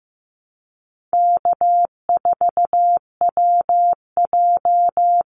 Morse code, heard as a keyed tone, about 15 wpm, 700 Hz. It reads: K4WJ
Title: Easy Call Signs Repeated 2x - 15wpm